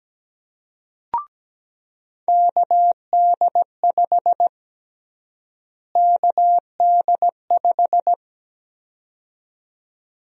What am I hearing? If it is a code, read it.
Morse code, heard as a keyed tone, about 17 wpm, 700 Hz. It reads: KD5 KD5
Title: USA Call Sign Prefixes - Repeated 2x - 17wpm